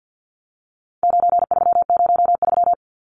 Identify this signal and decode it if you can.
Morse code, heard as a keyed tone, about 50 wpm, 700 Hz. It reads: 9303
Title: Numbers - 4-Digit 50wpm